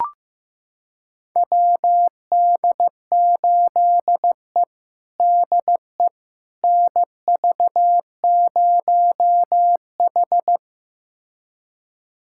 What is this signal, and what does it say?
Morse code, heard as a keyed tone, about 15 wpm, 700 Hz. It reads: WD8E DE NV0H
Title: USA Callsign DE Callsign - 4 Characters - 15wpm